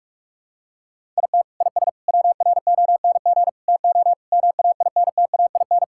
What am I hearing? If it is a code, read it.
Morse code, heard as a keyed tone, about 46 wpm, 700 Hz. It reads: IT IS WRONG TO MAINTAIN
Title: Sentences from Top 1000 Words - Review Part 5 - 45wpm